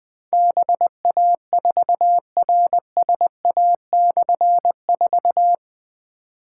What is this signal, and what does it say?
Morse code, heard as a keyed tone, about 20 wpm, 700 Hz. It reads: BA4RSA/4
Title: Difficult Call Signs 20wpm